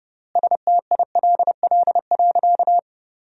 Morse code, heard as a keyed tone, about 32 wpm, 700 Hz. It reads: STILL.
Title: 3x Word-Spacing - Sentences from Top 500 Words - 30wpm